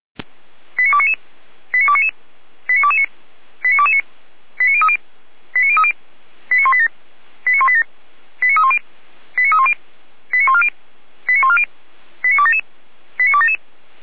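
Five-tone selective call (ZVEI), heard as quick runs of five tones, tones of about 70 ms. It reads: 89299 89299 89229 89229 89030 89030 89187 89187 89310 89310 89140 89140 89277 89277
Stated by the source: ZVEI five-tone selective call received with an SDR.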